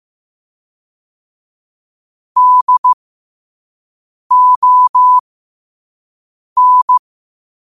Morse code, heard as a keyed tone, about 15 wpm, spaced as for 5 wpm, 1000 Hz. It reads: DON
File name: qotdfiles / 5wpm